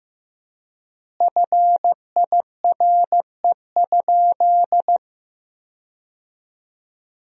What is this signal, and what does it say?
Morse code, heard as a keyed tone, about 16 wpm, 700 Hz. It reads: FIRE?
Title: 3x Word-Spacing - Sentences from Top 400 Words - 15wpm